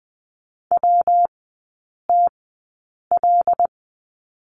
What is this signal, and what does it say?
Morse code, heard as a keyed tone, about 20 wpm, spaced as for 8 wpm, 700 Hz. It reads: WTL
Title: 121 Lesson 20e - Number 3 - Three Characters 20wpm